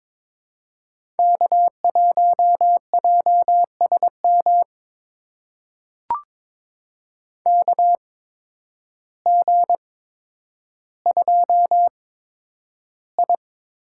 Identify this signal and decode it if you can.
Morse code, heard as a keyed tone, about 22 wpm, spaced as for 10 wpm, 700 Hz. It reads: K1JSM KG2I
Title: USA Call Signs - 8x Character Spacing - 22wpm